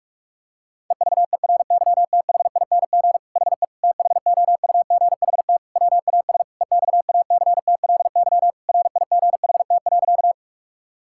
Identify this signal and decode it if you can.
Morse code, heard as a keyed tone, about 46 wpm, 700 Hz. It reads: EVERYTHING HE THOUGHT WAS EXACTLY RIGHT.
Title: Sentences from Top 1000 Words - Review Part 6 - 45wpm